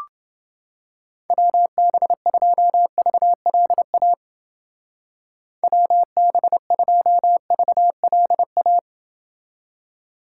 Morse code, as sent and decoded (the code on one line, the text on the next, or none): .-- -... ..--- ...- .-.. .- / .-- -... ..--- ...- .-.. .-
WB2VLA WB2VLA